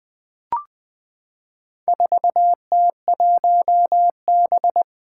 Morse code, heard as a keyed tone, about 20 wpm, 700 Hz. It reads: 4T1B
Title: Four Challenging Letter-Number Combinations 20wpm